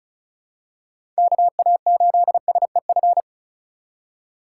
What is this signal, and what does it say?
Morse code, heard as a keyed tone, about 36 wpm, 700 Hz. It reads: KA8SEF
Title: USA Call Signs - 6 Characters - 4x Character Spacing - 35wpm